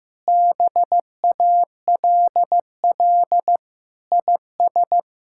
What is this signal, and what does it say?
Morse code, heard as a keyed tone, about 16 wpm, 700 Hz. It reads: BALL IS
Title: Sentences from Top 1000 Words - Review Part 6 - 15wpm